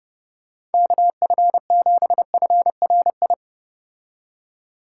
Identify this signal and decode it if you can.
Morse code, heard as a keyed tone, about 30 wpm, 700 Hz. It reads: KF7FRI